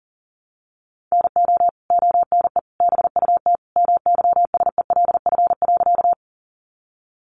Morse code, heard as a keyed tone, about 40 wpm, 700 Hz. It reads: NO ONE BUT MYSELF.
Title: Sentences from Top 1300 Words 40wpm